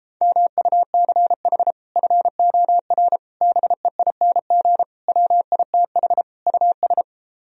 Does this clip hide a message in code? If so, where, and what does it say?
Yes: Morse code from 0.2 to 7.0 s, heard as a keyed tone, about 33 wpm, 700 Hz. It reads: MUCH FOR BEING WITH US